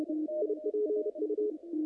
Morse code, heard as a keyed tone, about 30 wpm, 420 Hz, with radio noise. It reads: IPU
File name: contests / naqp-cw-k5tr-op-k5pi- -2300